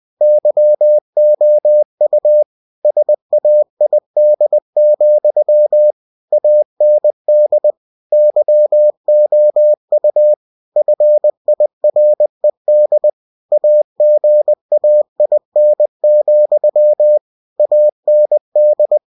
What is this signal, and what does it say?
Morse code, heard as a keyed tone, about 20 wpm, 600 Hz. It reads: YOU SAID, AND YOU FIRED AGAIN, AND